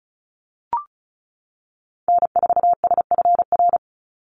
Morse code, heard as a keyed tone, about 34 wpm, 700 Hz. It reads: N4SFR
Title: NA SOTA Activator Callsigns 35wpm